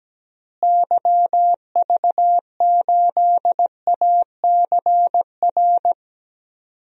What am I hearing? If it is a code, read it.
Morse code, heard as a keyed tone, about 17 wpm, 700 Hz. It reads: YV8ACR
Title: DXCC-100 Call Signs - 6 Characters - Repeated 3x - 17wpm